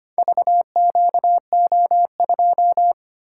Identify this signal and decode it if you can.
Morse code, heard as a keyed tone, about 25 wpm, 700 Hz. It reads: VQO2
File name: practice / 25wpm / Four Letter-Number Combinations - 25wpm